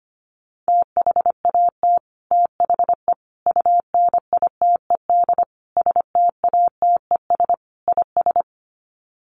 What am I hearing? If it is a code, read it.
Morse code, heard as a keyed tone, about 25 wpm, 700 Hz. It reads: THAT THE UNITED STATES IS
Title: Sets of 5 Words Encore 25wpm